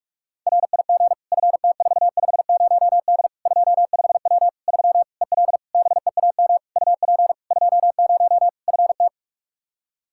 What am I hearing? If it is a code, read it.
Morse code, heard as a keyed tone, about 41 wpm, 700 Hz. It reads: RIG FT450D 25W 3 EL BEAM UP 20 FT
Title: Practice QSO - Exchange 2 - Info - 45wpm